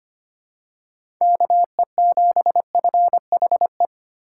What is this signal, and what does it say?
Morse code, heard as a keyed tone, about 25 wpm, 700 Hz. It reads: KE7FHE
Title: USA Call Signs - 6 Characters - 3x Character Spacing - 25wpm